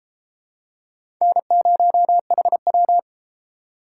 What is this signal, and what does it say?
Morse code, heard as a keyed tone, about 30 wpm, 700 Hz. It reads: N0HW